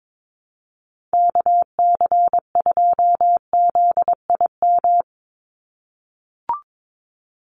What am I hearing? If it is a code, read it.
Morse code, heard as a keyed tone, about 22 wpm, 700 Hz. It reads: KC2ZIM
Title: USA Call Signs 22wpm